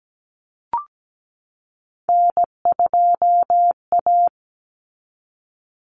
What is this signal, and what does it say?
Morse code, heard as a keyed tone, about 17 wpm, 700 Hz. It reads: N2A
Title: Common Cut Numbers - 3-Digit 17wpm